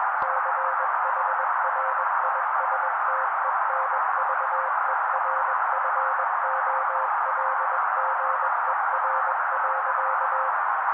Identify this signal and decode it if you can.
Morse code, heard as a keyed tone, about 21 wpm, 530 Hz. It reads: CHRISTENVERFOLGER.